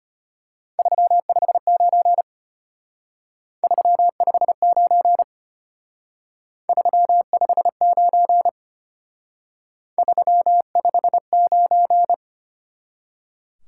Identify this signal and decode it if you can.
Morse code, heard as a keyed tone, about 32 wpm, 700 Hz. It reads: 359 359 359 359